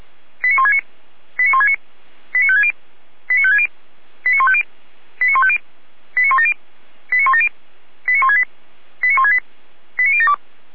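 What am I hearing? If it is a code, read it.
Five-tone selective call (ZVEI), heard as quick runs of five tones, tones of about 70 ms. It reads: 89169 89169 89560 89560 89150 89150 89170 89170 89168 89168 89072
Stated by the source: ZVEI five-tone selective call 89223 received by SDR, recorded by the Orlamünde fire brigade (feuerwehr-orlamuende).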